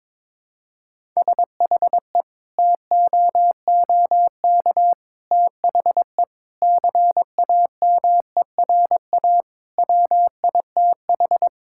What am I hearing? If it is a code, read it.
Morse code, heard as a keyed tone, about 21 wpm, 700 Hz. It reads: SHE TOOK THE CAMERA WITH
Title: Sentences from Top 1000 Words - Review Part 5 - 22wpm